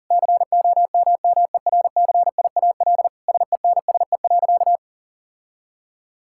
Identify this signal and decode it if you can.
Morse code, heard as a keyed tone, about 40 wpm, 700 Hz. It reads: COMMERCIAL SENSE.
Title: Sentences from Top 1000 Words - Review Part 5 - 40wpm